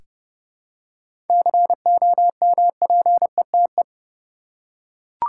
Morse code, heard as a keyed tone, about 30 wpm, 700 Hz. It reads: COMPETE
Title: Seven Letter Words - 30wpm